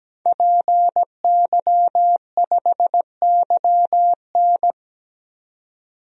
Morse code, heard as a keyed tone, about 17 wpm, 700 Hz. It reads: PY5YN
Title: DXCC-100 Call Signs - 5 Characters - 17wpm